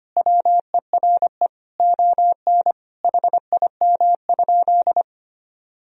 Morse code, heard as a keyed tone, about 25 wpm, 700 Hz. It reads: WERE ON HIM?
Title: Sentences from Top 1000 Words - Review Part 2 - 25wpm